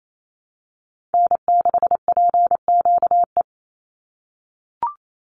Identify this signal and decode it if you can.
Morse code, heard as a keyed tone, about 28 wpm, 700 Hz. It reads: N6PQE
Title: USA Call Signs 28wpm